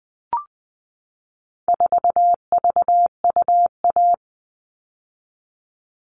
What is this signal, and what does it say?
Morse code, heard as a keyed tone, about 20 wpm, 700 Hz. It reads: 4VUA
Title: All Cut Numbers - 4-Digit 20wpm